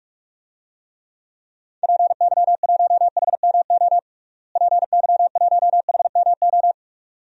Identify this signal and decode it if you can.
Morse code, heard as a keyed tone, about 46 wpm, 700 Hz. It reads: PY1HMO PY1HMO
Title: DXCC-100 Call Signs - 6 Characters - Repeated 2x - 45wpm